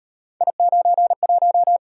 Morse code, heard as a keyed tone, about 34 wpm, 700 Hz. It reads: I91